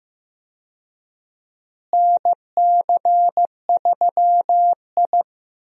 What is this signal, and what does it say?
Morse code, heard as a keyed tone, about 15 wpm, 700 Hz. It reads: NC3I
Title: USA Call Signs - 4 Characters - Repeated 2x - 15wpm